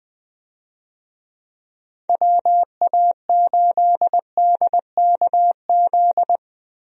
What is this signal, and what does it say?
Morse code, heard as a keyed tone, about 21 wpm, 700 Hz. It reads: WA8DKZ
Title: USA Call Signs - 6 Characters - 20wpm